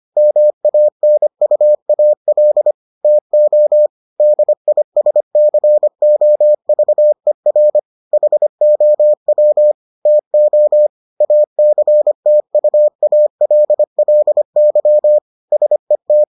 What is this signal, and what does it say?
Morse code, heard as a keyed tone, about 25 wpm, 600 Hz. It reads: MANUAL TO DISCOVER HOW TO ACTUALLY SET